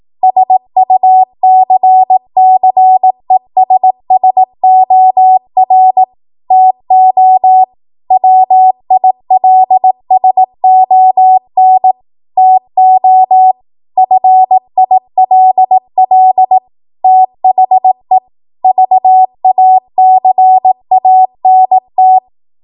Morse code, heard as a keyed tone, about 18 wpm, 750 Hz. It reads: SUCCESSOR TO WILSON TO FILL THE VACANT
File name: Archive / 18 WPM / 18WPM